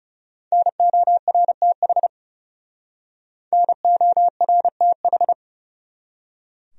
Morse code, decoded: NORTH NORTH

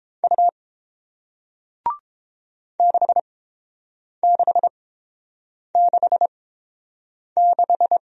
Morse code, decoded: U 6666

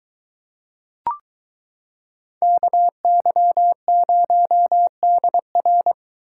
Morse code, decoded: KY0DR

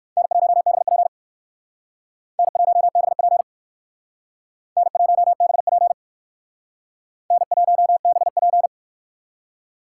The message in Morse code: N1BP N1BP N1BP N1BP

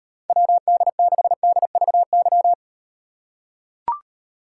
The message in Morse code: WD6DVY

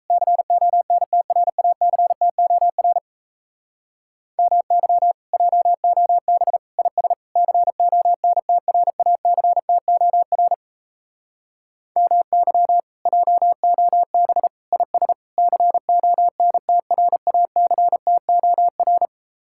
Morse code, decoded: CONTRACTOR MY JOB IS CONTRACTOR MY JOB IS CONTRACTOR

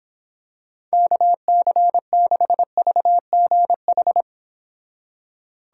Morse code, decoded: KC6VGH

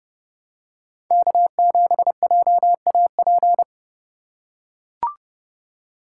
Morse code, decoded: K7JAP

